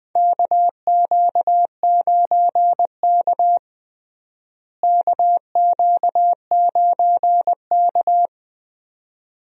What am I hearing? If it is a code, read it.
Morse code, heard as a keyed tone, about 20 wpm, 700 Hz. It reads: KQ9K KQ9K
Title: USA Call Signs - 4 Characters - Repeated 2x - 20wpm